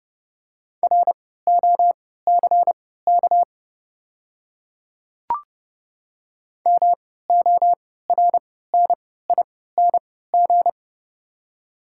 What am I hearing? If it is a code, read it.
Morse code, heard as a keyed tone, about 30 wpm, spaced as for 17 wpm, 700 Hz. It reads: ROCK MORNING
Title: 3x Character Spacing - Top 500 Words - 30wpm